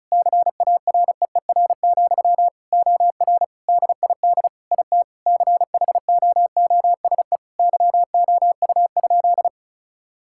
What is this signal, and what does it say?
Morse code, heard as a keyed tone, about 36 wpm, 700 Hz. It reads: CAREER, OR DID IT CHOOSE YOU?
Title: Sentences from Top 1000 Words - Review Part 6 - 35wpm